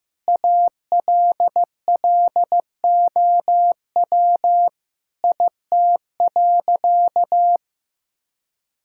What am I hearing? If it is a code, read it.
Morse code, heard as a keyed tone, about 15 wpm, 700 Hz. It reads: ALLOW IT.
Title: Sentences from Top 1000 Words - Review Part 5 - 15wpm